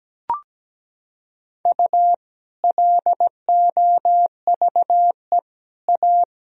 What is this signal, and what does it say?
Morse code, heard as a keyed tone, about 17 wpm, 700 Hz. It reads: U LOVE A
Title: Sentences from Top 500 Words and Abbreviations 17wpm